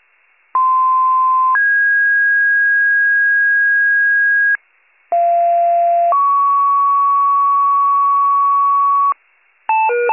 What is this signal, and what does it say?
Two-tone page, heard as a one-second tone, then a three-second tone, about 1040 then 1650 Hz, about 690 then 1120 Hz, with noise. It is heard from 0.5 to 9.1 s.